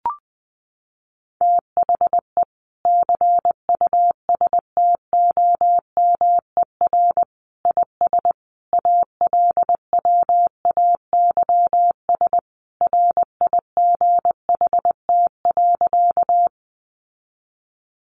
Morse code, heard as a keyed tone, about 20 wpm, 700 Hz. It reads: THE CUSTOMER IS ALWAYS RIGHT.
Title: Sentences from Top 1100 Words 20wpm